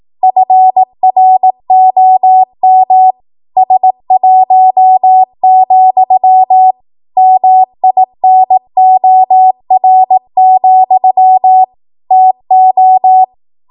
Morse code, heard as a keyed tone, about 18 wpm, 750 Hz. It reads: FROM S1, MINOR, TO
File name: Morse / 18WPM